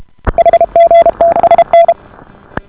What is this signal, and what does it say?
Morse code, heard as a keyed tone, about 33 wpm, 680 Hz. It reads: HGTSN